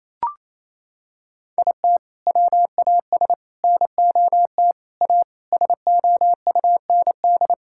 Morse code, heard as a keyed tone, about 27 wpm, 700 Hz. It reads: IT WAS NOT A SOUND
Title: Sentences from Top 1000 Words - Review Part 6 - 28wpm